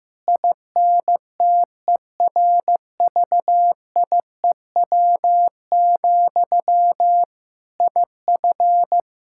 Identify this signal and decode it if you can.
Morse code, heard as a keyed tone, about 15 wpm, 700 Hz. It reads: INTERVIEW, IF
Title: Sentences from Top 1600 Words 15wpm